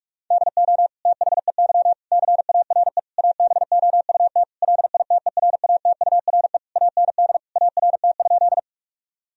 Morse code, heard as a keyed tone, about 42 wpm, 700 Hz. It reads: DO THEY CARE ABOUT LITERATURE AND ART?
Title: Sentences from Top 1000 Words - Review Part 3 - 45wpm